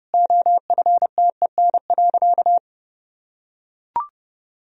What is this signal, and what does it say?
Morse code, heard as a keyed tone, about 32 wpm, 700 Hz. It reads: OFTEN.
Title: Sentences from Top 1000 Words - Review Part 1 - 30wpm